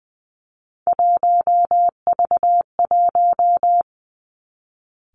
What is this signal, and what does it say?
Morse code, heard as a keyed tone, about 20 wpm, 700 Hz. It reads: 1V1
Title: Three Challenging Letter-Number Combinations 20wpm